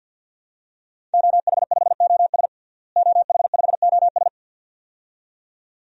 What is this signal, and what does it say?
Morse code, heard as a keyed tone, about 51 wpm, 700 Hz. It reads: OH5OS OH5OS
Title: DXCC-100 Call Signs - 5 Characters - Repeated 2x - 50wpm